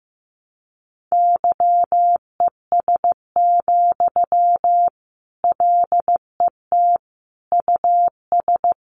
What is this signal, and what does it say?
Morse code, heard as a keyed tone, about 14 wpm, 700 Hz. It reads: YES, LET US
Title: Sentences from Top 1200 Words 15wpm